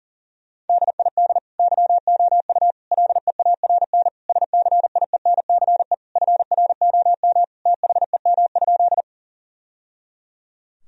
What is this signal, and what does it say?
Morse code, heard as a keyed tone, about 40 wpm, 700 Hz. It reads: DID YOU LEARN SCIENCE FROM THEM?
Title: Sentences from Top 1000 Words - Review Part 2 - 40wpm